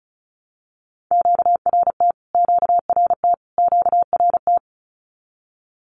Morse code, heard as a keyed tone, about 34 wpm, 700 Hz. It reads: QRT QRT QRT
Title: Q-Codes Repeated 3x 35wpm